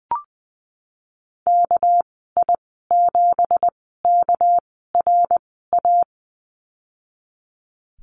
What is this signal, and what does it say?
Morse code, heard as a keyed tone, about 20 wpm, spaced as for 15 wpm, 700 Hz. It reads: KI7KRA
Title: 213 Lesson 34g - Number 8 - Call Signs 20wpm